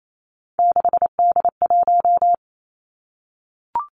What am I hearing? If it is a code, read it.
Morse code, heard as a keyed tone, about 28 wpm, 700 Hz. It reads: 6D1
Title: Three Challenging Letter-Number Combinations 28wpm